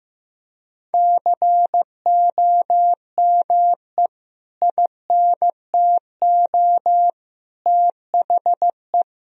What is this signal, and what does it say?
Morse code, heard as a keyed tone, about 15 wpm, 700 Hz. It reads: COME INTO THE
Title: Sentences from Top 1000 Words - Review Part 4 - 15wpm